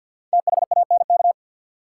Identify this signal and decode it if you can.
Morse code, heard as a keyed tone, about 52 wpm, 700 Hz. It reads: THANK
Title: Sets of 2 Words - New Word Drill - 50wpm